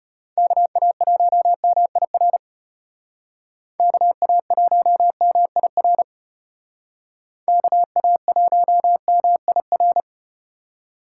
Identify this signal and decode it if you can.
Morse code, heard as a keyed tone, about 34 wpm, 700 Hz. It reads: KA1MIR KA1MIR KA1MIR